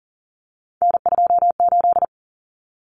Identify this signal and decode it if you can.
Morse code, heard as a keyed tone, about 40 wpm, 700 Hz. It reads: N28
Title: Common Cut Numbers - 3-Digit 40wpm